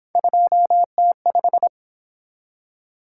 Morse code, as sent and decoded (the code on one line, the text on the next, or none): ..--- - .....
2T5